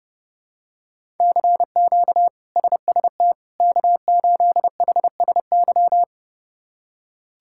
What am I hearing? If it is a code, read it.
Morse code, heard as a keyed tone, about 30 wpm, 700 Hz. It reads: CQ SST K8HSY
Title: K1USN SST QSO Practice - 30wpm